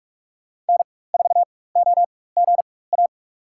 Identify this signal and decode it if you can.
Morse code, heard as a keyed tone, about 46 wpm, spaced as for 21 wpm, 700 Hz. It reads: N4OGA
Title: USA Call Signs - 5 Characters - 4x Character Spacing - 45wpm